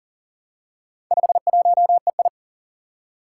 Morse code, heard as a keyed tone, about 40 wpm, 700 Hz. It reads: 51EI
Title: Four Challenging Letter-Number Combinations 40wpm